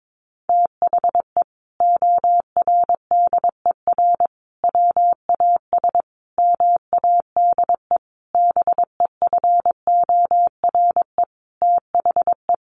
Morse code, heard as a keyed tone, about 22 wpm, 700 Hz. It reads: THE ORDER WAS MADE BEFORE THE
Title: Sentences from Top 1500 Words 22wpm